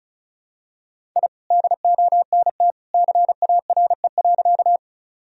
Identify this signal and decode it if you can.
Morse code, heard as a keyed tone, about 34 wpm, 700 Hz. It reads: I DONT CARE.
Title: Sentences from Top 1000 Words - Review Part 1 - 35wpm